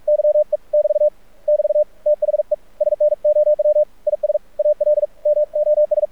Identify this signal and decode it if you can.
Morse code, heard as a keyed tone, about 46 wpm, 600 Hz. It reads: YE <BT> <BT> THE SNOW IS ALMOS
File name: archived / CRSnet